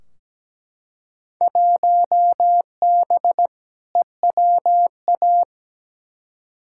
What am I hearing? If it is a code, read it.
Morse code, heard as a keyed tone, about 17 wpm, 700 Hz. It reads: 1B EWA
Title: Field Day Exchanges Encore - 17wpm